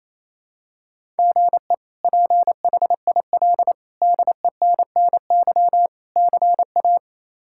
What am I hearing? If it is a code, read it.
Morse code, heard as a keyed tone, about 28 wpm, 700 Hz. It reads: GE PHIL DENNY CA
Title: K1USN SST QSO Practice - 28wpm